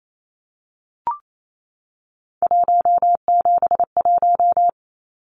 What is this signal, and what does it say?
Morse code, heard as a keyed tone, about 28 wpm, 700 Hz. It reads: 171
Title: Numbers - 3-Digit 28wpm